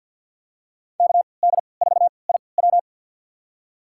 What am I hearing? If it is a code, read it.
Morse code, heard as a keyed tone, about 52 wpm, spaced as for 29 wpm, 700 Hz. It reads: KD4IW
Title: USA Call Signs - 5 Characters - 3x Character Spacing - 50wpm